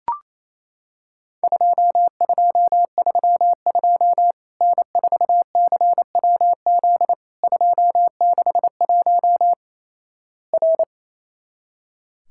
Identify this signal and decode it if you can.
Morse code, heard as a keyed tone, about 28 wpm, 700 Hz. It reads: 2232 N4CWZ 261 R
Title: Worked All Europe DX Contest - QTC Exchange - 28wpm